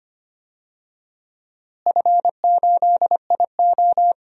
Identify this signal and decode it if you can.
Morse code, heard as a keyed tone, about 25 wpm, 700 Hz. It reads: F8IO
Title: DXCC-100 Call Signs - 4 Characters - Repeated 3x - 25wpm